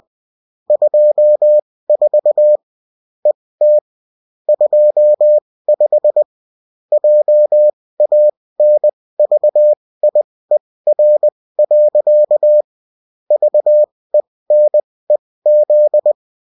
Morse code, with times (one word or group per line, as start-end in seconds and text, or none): 0.7-2.6 24
3.3-3.8 ET
4.5-6.2 25
6.9-12.6 JANVIER.
13.3-16.1 VENEZ